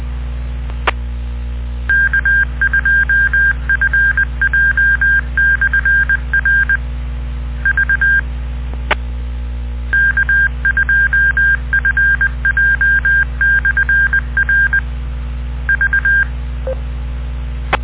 Morse code, heard as a keyed tone, about 20 wpm, 1600 Hz. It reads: K2FJ/R V K2FJ/R V